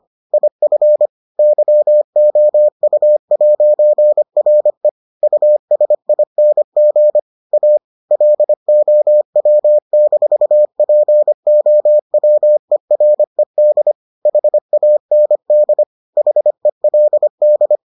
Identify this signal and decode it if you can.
Morse code, heard as a keyed tone, about 25 wpm, 600 Hz. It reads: IF YOU'RE USING A LOW-POWERED HAND HELD